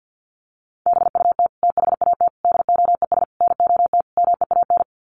Morse code, heard as a keyed tone, about 50 wpm, 700 Hz. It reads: BUT THAT DOES NOT MEAN
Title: Sets of 5 Words 50wpm